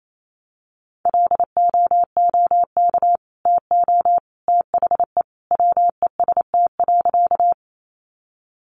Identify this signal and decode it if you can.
Morse code, heard as a keyed tone, about 28 wpm, 700 Hz. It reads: LOOK TO THE WEST.
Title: Sentences from Top 400 Words 28wpm